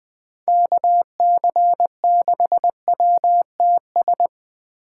Morse code, heard as a keyed tone, about 20 wpm, 700 Hz. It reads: KC6WTS